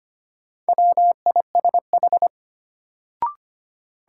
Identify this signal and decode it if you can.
Morse code, heard as a keyed tone, about 25 wpm, 700 Hz. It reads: WISH
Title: Sentences from Top 1600 Words - New Word Drill Alt - 25wpm